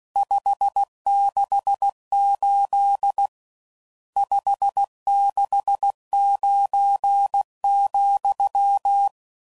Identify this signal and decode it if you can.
Morse code, heard as a keyed tone, about 16 wpm, 800 Hz. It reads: 568 569,
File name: CW Code Groups @15WPM / RSTs / Various RST Nos